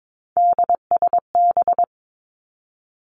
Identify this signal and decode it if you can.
Morse code, heard as a keyed tone, about 22 wpm, 700 Hz. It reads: DSB
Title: Three Challenging Letter-Number Combinations 22wpm